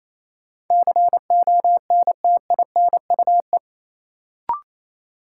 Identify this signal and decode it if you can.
Morse code, heard as a keyed tone, about 28 wpm, 700 Hz. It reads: CONTINUE
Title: Sentences from Top 1100 Words - New Word Drill Alt - 28wpm